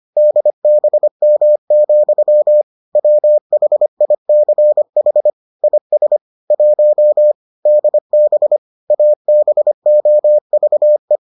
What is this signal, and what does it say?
Morse code, heard as a keyed tone, about 25 wpm, 600 Hz. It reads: DBM, WHICH IS 1 DB ABOVE